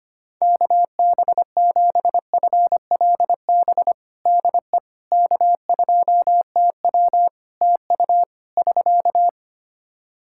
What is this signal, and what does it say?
Morse code, heard as a keyed tone, about 25 wpm, 700 Hz. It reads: KB7FLB DE K2TW TU <SK>
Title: Practice QSO - Exchange 4 - Ending - 25wpm